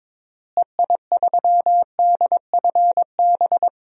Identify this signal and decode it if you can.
Morse code, heard as a keyed tone, about 22 wpm, 700 Hz. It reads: EI3DFB
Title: DXCC-100 Call Signs - 6 Characters - Repeated 3x - 22wpm